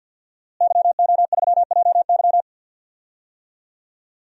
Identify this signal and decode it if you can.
Morse code, heard as a keyed tone, about 52 wpm, 700 Hz. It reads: YO3JY
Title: DXCC-100 Call Signs - 5 Characters - Repeated 3x - 50wpm